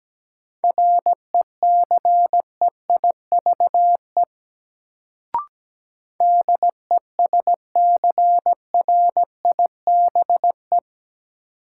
Morse code, heard as a keyed tone, about 17 wpm, 700 Hz. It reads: RECEIVE DESCRIBE
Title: Sentences from Top 1000 Words - New Word Drill - 17wpm